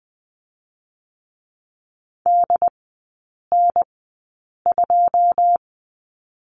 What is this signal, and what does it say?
Morse code, heard as a keyed tone, about 20 wpm, spaced as for 8 wpm, 700 Hz. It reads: DN2
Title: 100 Lesson 17e - Letter u - Three Characters 20wpm